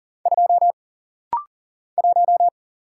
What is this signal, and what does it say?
Morse code, heard as a keyed tone, about 40 wpm, 700 Hz. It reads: 2 1
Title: Single Letter-Number - Rapid-Fire 40wpm